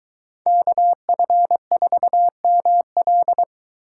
Morse code, heard as a keyed tone, about 23 wpm, 700 Hz. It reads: KF4ML